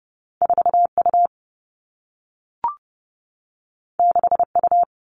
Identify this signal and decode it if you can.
Morse code, heard as a keyed tone, about 30 wpm, 700 Hz. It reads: 4 U 6 U
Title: Two Challenging Letter-Number Combinations 30wpm